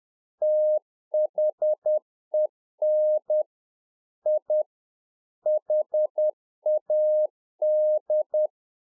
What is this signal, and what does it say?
Morse code, heard as a keyed tone, about 10 wpm, 610 Hz. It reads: THEN I HAD